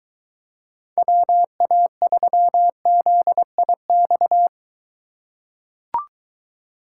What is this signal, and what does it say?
Morse code, heard as a keyed tone, about 23 wpm, 700 Hz. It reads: WA3ZIX